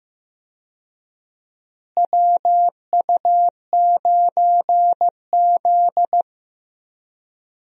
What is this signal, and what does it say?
Morse code, heard as a keyed tone, about 15 wpm, 700 Hz. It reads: WU9Z
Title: USA Call Signs - 4 Characters - Repeated 3x - 15wpm